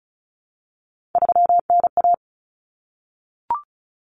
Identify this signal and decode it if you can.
Morse code, heard as a keyed tone, about 34 wpm, 700 Hz. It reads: 3NA
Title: Common Cut Numbers - 3-Digit 35wpm